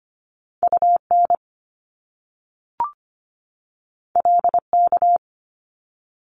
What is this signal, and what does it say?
Morse code, heard as a keyed tone, about 25 wpm, 700 Hz. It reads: UN LK
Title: Top Two Letter Combinations in Words 25wpm